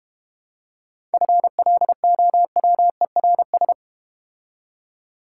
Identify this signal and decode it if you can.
Morse code, heard as a keyed tone, about 30 wpm, 700 Hz. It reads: FLOWERS
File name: Easy Words - Speed-Racing with Hints - 25wpm